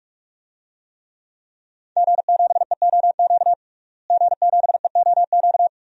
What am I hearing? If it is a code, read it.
Morse code, heard as a keyed tone, about 42 wpm, 700 Hz. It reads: G7EOQ G7EOQ
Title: DXCC-100 Call Signs - 5 Characters - Repeated 2x - 45wpm